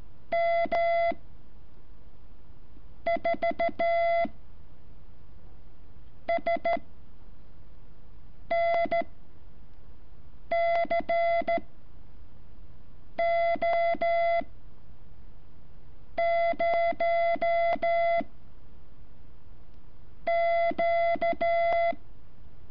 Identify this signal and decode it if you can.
Morse code, heard as a keyed tone, about 13 wpm, 680 Hz. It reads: M 4 S N C O 0 Q
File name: curso / ejercicio 15